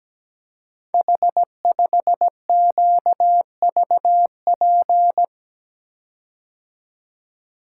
Morse code, heard as a keyed tone, about 17 wpm, 700 Hz. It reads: H5QVP
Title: Code Groups - 5 Letter:Numbers - 17wpm